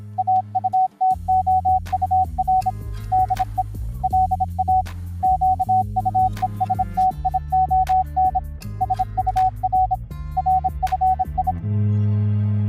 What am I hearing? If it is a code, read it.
Morse code, heard as a keyed tone, about 26 wpm, 750 Hz. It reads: AUTOUR DE LA QUESTION SUR RFI